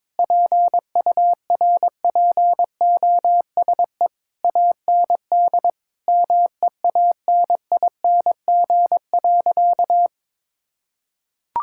Morse code, heard as a keyed tone, about 21 wpm, 700 Hz. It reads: PURPOSE AND MEANING.
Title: Sentences from Top 1000 Words - Review Part 6 - 22wpm